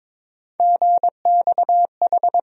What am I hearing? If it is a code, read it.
Morse code, heard as a keyed tone, about 22 wpm, 700 Hz. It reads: GXH
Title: USA Call Sign Postfixes - 3 Letters 22wpm